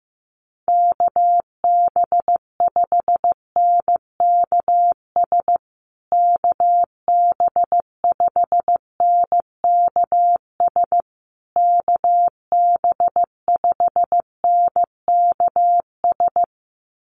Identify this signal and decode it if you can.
Morse code, heard as a keyed tone, about 15 wpm, 700 Hz. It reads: KB5NKS KB5NKS KB5NKS
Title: USA Call Signs Repeated 3x 15wpm